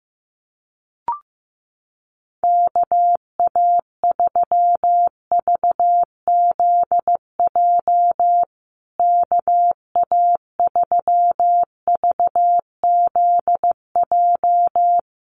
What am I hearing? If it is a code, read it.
Morse code, heard as a keyed tone, about 15 wpm, 700 Hz. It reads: KA3VZJ KA3VZJ
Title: USA Call Signs Repeated 3x 15wpm